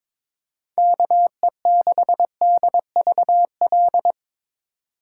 Morse code, heard as a keyed tone, about 25 wpm, 700 Hz. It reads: KE6DVL
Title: Speed Racing - USA 2x3 Call Signs 22wpm